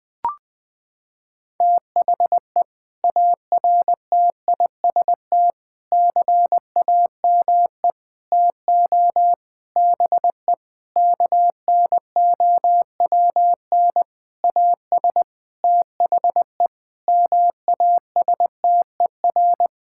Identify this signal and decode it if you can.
Morse code, heard as a keyed tone, about 21 wpm, 700 Hz. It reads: THE ARTIST CAME TO BE KNOWN AS THE MASTER
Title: Sentences with 6-Word Phrases - 20wpm